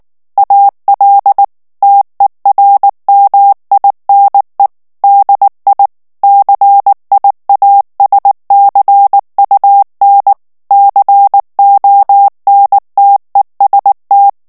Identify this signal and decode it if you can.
Morse code, heard as a keyed tone, about 19 wpm, 800 Hz. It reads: AL TERMINE DI CIASCUN CONTEST
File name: morse mp3 / ITALIEN texte clair 2 vitesse 18 mots minute